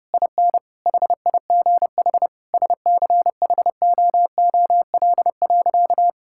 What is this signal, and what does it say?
Morse code, heard as a keyed tone, about 30 wpm, 700 Hz. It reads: IN HIGH SCHOOL.
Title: Sentences from Top 1000 Words - Review Part 6 - 30wpm